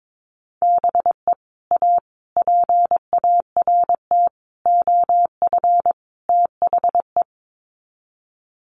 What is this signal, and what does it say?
Morse code, heard as a keyed tone, about 22 wpm, 700 Hz. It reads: BE A PART OF THE
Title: Sets of 5 Words Encore 22wpm